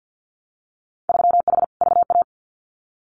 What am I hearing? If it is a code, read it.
Morse code, heard as a keyed tone, about 50 wpm, 700 Hz. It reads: 3H VA
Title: Winter Field Day Exchanges 50wpm